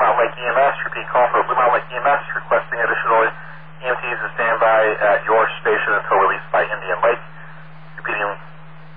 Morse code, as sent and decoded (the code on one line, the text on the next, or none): none